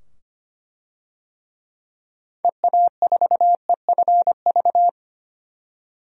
Morse code, heard as a keyed tone, about 25 wpm, 700 Hz. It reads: EA4EFV
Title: DXCC-100 Call Signs - 6 Characters - 25wpm